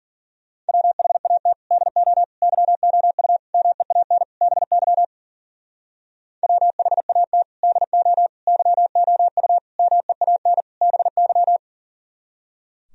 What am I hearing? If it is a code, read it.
Morse code, heard as a keyed tone, about 51 wpm, 700 Hz. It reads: WHAT DO YOU MEAN BY WHAT DO YOU MEAN BY